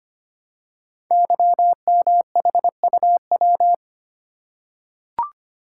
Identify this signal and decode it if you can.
Morse code, heard as a keyed tone, about 25 wpm, 700 Hz. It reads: YMHUW
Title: Code Groups - 5 Letter:Numbers - 25wpm